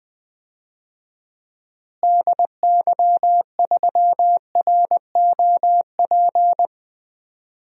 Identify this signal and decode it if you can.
Morse code, heard as a keyed tone, about 20 wpm, 700 Hz. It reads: DY3ROP
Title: DXCC-100 Call Signs - 6 Characters - Repeated 2x - 20wpm